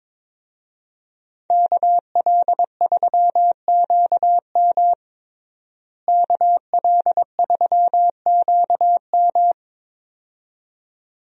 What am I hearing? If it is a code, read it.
Morse code, heard as a keyed tone, about 22 wpm, 700 Hz. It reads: KL3QM KL3QM
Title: USA Call Signs - 5 Characters - Repeated 2x - 22wpm